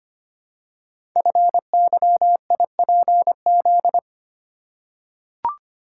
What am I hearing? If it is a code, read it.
Morse code, heard as a keyed tone, about 25 wpm, 700 Hz. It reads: FYIPZ
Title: Code Groups - 5 Letter:Numbers - 25wpm